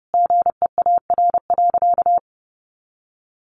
Morse code, heard as a keyed tone, about 30 wpm, 700 Hz. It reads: GEAR.
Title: Sentences from Top 1400 Words 30wpm